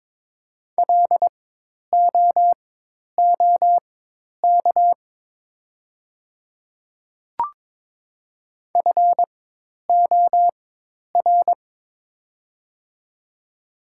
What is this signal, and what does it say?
Morse code, heard as a keyed tone, about 22 wpm, spaced as for 10 wpm, 700 Hz. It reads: LOOK FOR
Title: 4x Character Spacing - Top 100 Words - 22wpm